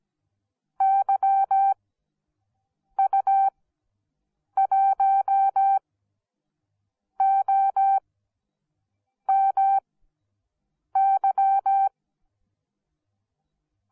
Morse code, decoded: YU1OMY

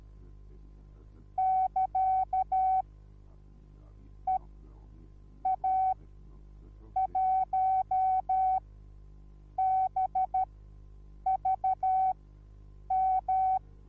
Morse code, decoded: <KA> EA1BVM

